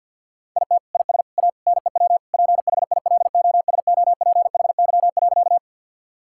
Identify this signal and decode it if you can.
Morse code, heard as a keyed tone, about 49 wpm, 700 Hz. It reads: IT IS A NEW PHILOSOPHY.
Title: Sentences from Top 1000 Words - Review Part 6 - 50wpm